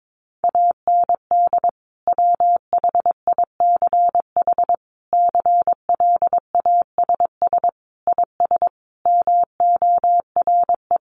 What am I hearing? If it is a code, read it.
Morse code, heard as a keyed tone, about 22 wpm, 700 Hz. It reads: AND WHICH CLASS IS MORE
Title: Sentences from Top 1400 Words 22wpm